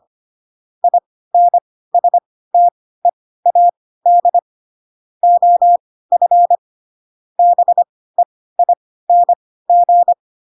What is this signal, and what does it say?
Morse code, heard as a keyed tone, about 25 wpm, spaced as for 16 wpm, 700 Hz. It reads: INSTEAD OF BEING
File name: Pinocchio / Pinocchio ch4 10 25 ik1hge